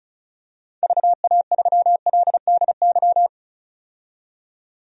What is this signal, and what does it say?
Morse code, heard as a keyed tone, about 35 wpm, 700 Hz. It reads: VA3LDY
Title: DXCC-100 Call Signs - 6 Characters - 35wpm